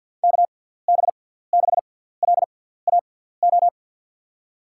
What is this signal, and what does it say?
Morse code, heard as a keyed tone, about 53 wpm, spaced as for 17 wpm, 700 Hz. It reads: KB6LAO
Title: USA Call Signs - 6 Characters - 6x Character Spacing - 50wpm